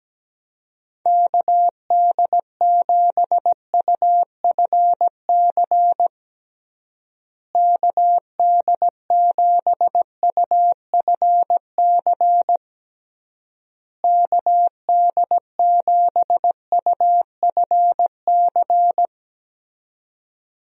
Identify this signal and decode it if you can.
Morse code, heard as a keyed tone, about 17 wpm, 700 Hz. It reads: KD7UFC KD7UFC KD7UFC
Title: USA Call Signs - 6 Characters - Repeated 3x - 17wpm